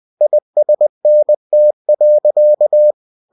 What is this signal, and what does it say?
Morse code, heard as a keyed tone, about 20 wpm, 600 Hz. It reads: ISNT.
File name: winnie-20wpm